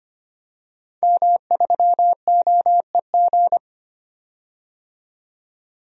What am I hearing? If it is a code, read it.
Morse code, heard as a keyed tone, about 25 wpm, 700 Hz. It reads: M3OEG
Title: DXCC-100 Call Signs - 5 Characters - 25wpm